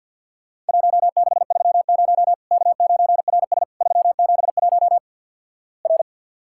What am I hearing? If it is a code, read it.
Morse code, heard as a keyed tone, about 50 wpm, 700 Hz. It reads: 1630 K9RS 371 R
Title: Worked All Europe DX Contest - QTC Exchange with Repeats - 50wpm